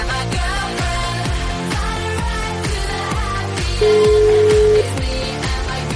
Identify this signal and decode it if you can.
Ringback tone (Europe, 425 Hz), with noise.